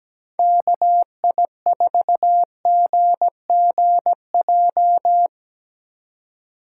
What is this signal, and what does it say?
Morse code, heard as a keyed tone, about 20 wpm, 700 Hz. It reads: KI4GGJ
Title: Speed Racing - USA Call Signs 17wpm